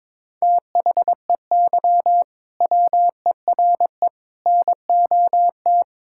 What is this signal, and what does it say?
Morse code, heard as a keyed tone, about 22 wpm, 700 Hz. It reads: THEY WERE NOT
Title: Sentences from Top 1000 Words - Review Part 6 - 22wpm